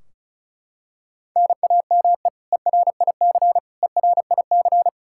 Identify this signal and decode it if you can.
Morse code, heard as a keyed tone, about 36 wpm, 700 Hz. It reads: NAME ERIC ERIC
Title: Practice QSO - Exchange 1 - Beginning - 35wpm